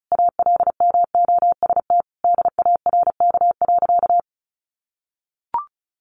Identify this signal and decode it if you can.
Morse code, heard as a keyed tone, about 34 wpm, 700 Hz. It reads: ALMOST DARK.
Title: Sentences from Top 700 Words 35wpm